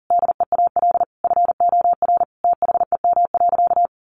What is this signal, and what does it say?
Morse code, heard as a keyed tone, about 40 wpm, 700 Hz. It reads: DEAL FOR THEM.
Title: Sentences from Top 800 Words 40wpm